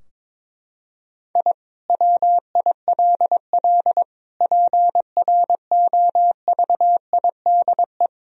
Morse code, heard as a keyed tone, about 22 wpm, 700 Hz. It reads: I WILL PROVIDE
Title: Sentences from Top 1000 Words - Review Part 3 - 22wpm